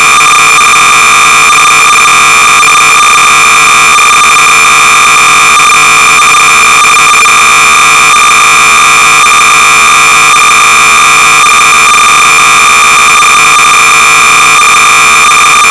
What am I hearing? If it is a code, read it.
Cassette data (Kansas City type).
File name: sorcerer / 300baud / LABYRINT Maze generator-(perhaps the original version)-BASIC-Sorcerer-300baud